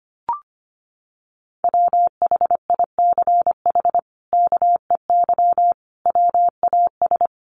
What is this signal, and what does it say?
Morse code, heard as a keyed tone, about 25 wpm, 700 Hz. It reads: WHICH KEY WAS
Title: Sentences from Top 600 Words 25wpm